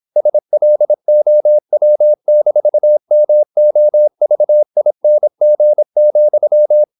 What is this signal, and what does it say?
Morse code, heard as a keyed tone, about 26 wpm, 600 Hz. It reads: SLOW-MOVING,
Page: Never 26wpm | CW med Gnister